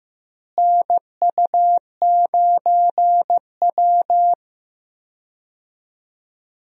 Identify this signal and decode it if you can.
Morse code, heard as a keyed tone, about 15 wpm, 700 Hz. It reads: NU9W
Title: USA Call Signs - 4 Characters - Repeated 3x - 15wpm